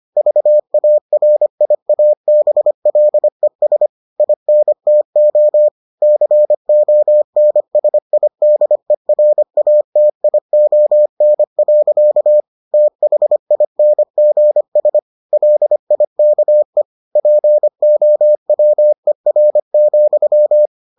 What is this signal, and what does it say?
Morse code, heard as a keyed tone, about 25 wpm, 600 Hz. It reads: VARIABLES INTO CONSIDERATION. THINGS LIKE POWER,